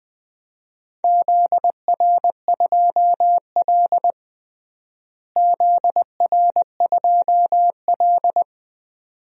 Morse code, heard as a keyed tone, about 20 wpm, 700 Hz. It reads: ZR2L ZR2L
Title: DXCC-100 Call Signs - 4 Characters - Repeated 2x - 20wpm